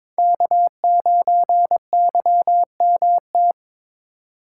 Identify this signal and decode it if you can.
Morse code, heard as a keyed tone, about 22 wpm, 700 Hz. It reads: K9YMT
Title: USA Call Signs - 5 Characters - 3x Character Spacing - 22wpm